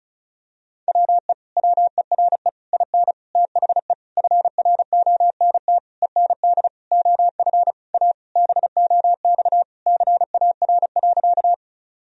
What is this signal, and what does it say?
Morse code, heard as a keyed tone, about 36 wpm, 700 Hz. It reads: WE WERE IN THE FRONT END OF A BOX CAR.
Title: Sentences from Top 1000 Words - Review Part 2 - 35wpm